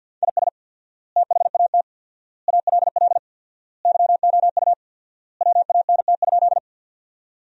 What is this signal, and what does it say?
Morse code, heard as a keyed tone, about 50 wpm, 700 Hz. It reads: IS THAT ALL YOU WANT?
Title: 4x Word-Spacing - Sentences from Top 200 Words - 50wpm